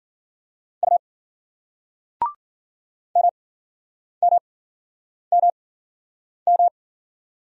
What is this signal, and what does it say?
Morse code, heard as a keyed tone, about 48 wpm, spaced as for 9 wpm, 700 Hz. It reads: U MMMM
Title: Speed-Racing Single Letter-Number 40wpm